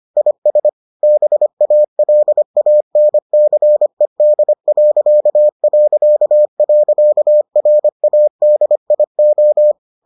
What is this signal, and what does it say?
Morse code, heard as a keyed tone, about 25 wpm, 600 Hz. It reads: IS BALANCED...RADIO